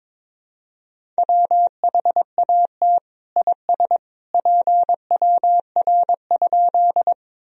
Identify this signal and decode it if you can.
Morse code, heard as a keyed tone, about 22 wpm, 700 Hz. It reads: WHAT IS PWR?
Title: Sentences from Top 500 Words and Abbreviations 22wpm